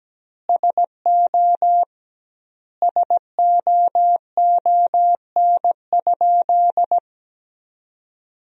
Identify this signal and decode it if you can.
Morse code, heard as a keyed tone, about 17 wpm, 700 Hz. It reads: SO SOON?
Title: 2x Word-Spacing - Sentences from Top 400 Words - 17wpm